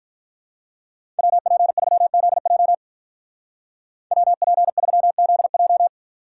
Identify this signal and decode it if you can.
Morse code, heard as a keyed tone, about 45 wpm, 700 Hz. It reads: WP3ZJ WP3ZJ